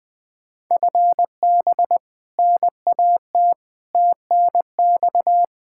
Morse code, heard as a keyed tone, about 21 wpm, 700 Hz. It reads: FB NAT TNX
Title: Practice QSO - Exchange 3 - More Info - 20wpm